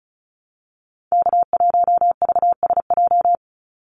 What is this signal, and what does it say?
Morse code, heard as a keyed tone, about 34 wpm, 700 Hz. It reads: K1VSJ
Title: Call Signs - 35wpm